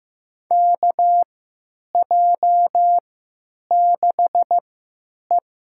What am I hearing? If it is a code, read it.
Morse code, heard as a keyed tone, about 15 wpm, spaced as for 9 wpm, 700 Hz. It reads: KJ6E
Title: USA Call Signs - 3x Character Spacing - 15wpm